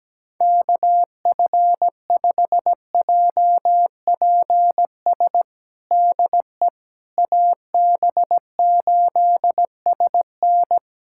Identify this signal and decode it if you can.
Morse code, heard as a keyed tone, about 17 wpm, 700 Hz. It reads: KF5JPS DE AB8SN
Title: USA Callsign DE Callsign 17wpm